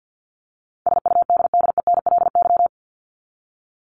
Morse code, heard as a keyed tone, about 50 wpm, 700 Hz. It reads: SUDDENLY
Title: LY Suffix - Top 100 Words with Base Words 50wpm